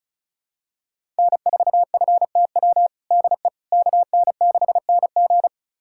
Morse code, heard as a keyed tone, about 34 wpm, 700 Hz. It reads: N4FTW DE KN6NG
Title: USA Callsign DE Callsign - 5 Characters - 35wpm